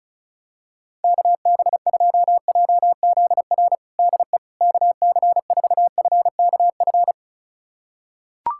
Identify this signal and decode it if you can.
Morse code, heard as a keyed tone, about 34 wpm, 700 Hz. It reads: KB2JZR DE KC4FKF
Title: USA Callsign DE Callsign - 6 Characters - 35wpm